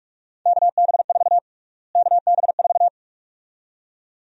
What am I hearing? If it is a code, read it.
Morse code, heard as a keyed tone, about 46 wpm, 700 Hz. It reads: KB4 KB4
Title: USA Call Sign Prefixes - Repeated 2x - 45wpm